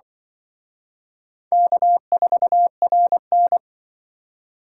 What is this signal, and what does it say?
Morse code, heard as a keyed tone, about 25 wpm, 700 Hz. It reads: K4RN
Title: Speed Racing - USA 1x2 Call Signs 20wpm